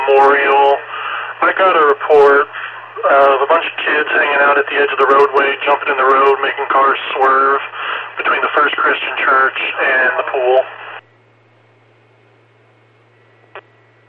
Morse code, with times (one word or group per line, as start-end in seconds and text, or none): none